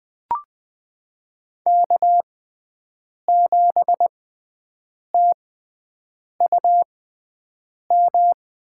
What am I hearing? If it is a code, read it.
Morse code, heard as a keyed tone, about 20 wpm, spaced as for 7 wpm, 700 Hz. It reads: K7TUM
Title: USA Call Signs - 5 Characters - 6x Character Spacing - 20wpm